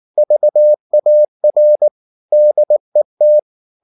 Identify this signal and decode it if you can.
Morse code, heard as a keyed tone, about 19 wpm, 600 Hz. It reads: VAR DET
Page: Ferie 19 wpm